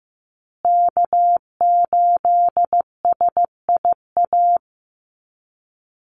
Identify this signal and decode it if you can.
Morse code, heard as a keyed tone, about 15 wpm, 700 Hz. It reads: K8SIA
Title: Call Signs Repeated 2x - 15wpm